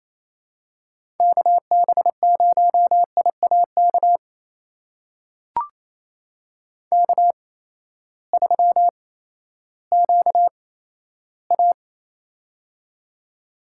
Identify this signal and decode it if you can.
Morse code, heard as a keyed tone, about 28 wpm, spaced as for 9 wpm, 700 Hz. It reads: KB0IA KK3QA
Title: USA Call Signs - 8x Character Spacing - 28wpm